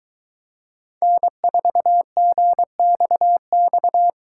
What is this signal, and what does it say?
Morse code, heard as a keyed tone, about 22 wpm, 700 Hz. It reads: N4GXX